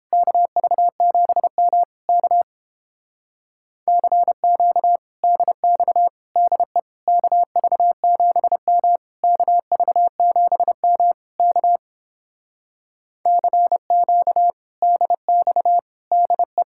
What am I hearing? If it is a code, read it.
Morse code, heard as a keyed tone, about 29 wpm, 700 Hz. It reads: KV7M K CQ DX DE KV7M KV7M K CQ DX DE